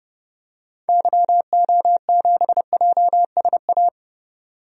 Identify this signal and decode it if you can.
Morse code, heard as a keyed tone, about 30 wpm, 700 Hz. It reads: YO7JSA
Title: DXCC-100 Call Signs - 6 Characters - 30wpm